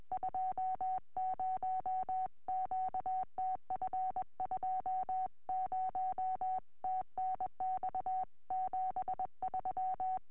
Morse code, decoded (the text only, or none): 20 QTF 20 TNX 73